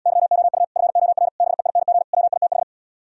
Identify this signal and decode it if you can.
Morse code, heard as a keyed tone, about 77 wpm, 700 Hz. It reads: YOU FOR BEING WITH